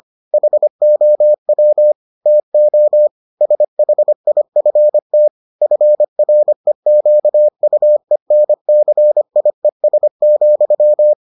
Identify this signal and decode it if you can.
Morse code, heard as a keyed tone, about 25 wpm, 600 Hz. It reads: HOW TO SHIFT FREQUENCIES,